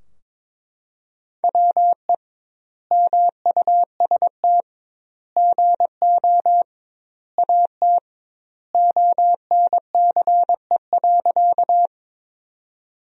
Morse code, heard as a keyed tone, about 22 wpm, 700 Hz. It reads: WE MUST GO AT ONCE.
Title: 2x Word-Spacing - Sentences from Top 300 Words - 22wpm